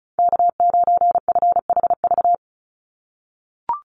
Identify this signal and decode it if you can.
Morse code, heard as a keyed tone, about 34 wpm, 700 Hz. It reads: K9FHV
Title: USA Call Signs 35wpm